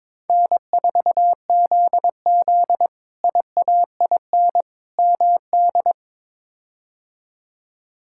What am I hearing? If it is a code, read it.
Morse code, heard as a keyed tone, about 22 wpm, 700 Hz. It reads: N4ZZ IAIN MD
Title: K1USN SST QSO Practice - 22wpm